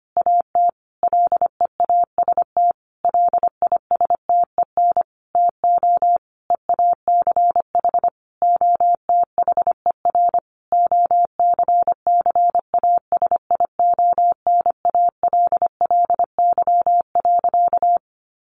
Morse code, heard as a keyed tone, about 25 wpm, 700 Hz. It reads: AT LEAST LISTEN TO EACH OTHER OCCASIONALLY.
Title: Sentences from Top 1300 Words 25wpm